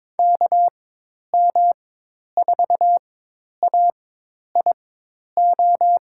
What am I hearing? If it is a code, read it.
Morse code, heard as a keyed tone, about 22 wpm, spaced as for 10 wpm, 700 Hz. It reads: KM4AIO
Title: USA Call Signs - 4x Character Spacing - 22wpm